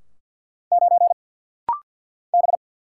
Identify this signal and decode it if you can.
Morse code, heard as a keyed tone, about 49 wpm, 700 Hz. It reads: 9 B
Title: Single Letter-Number Alt - Rapid-Fire 50wpm